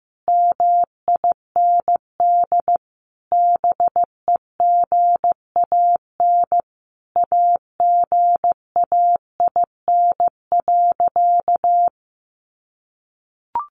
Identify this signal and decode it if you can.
Morse code, heard as a keyed tone, about 15 wpm, 700 Hz. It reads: MIND BEGAN AGAIN.
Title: Sentences from Top 1300 Words 15wpm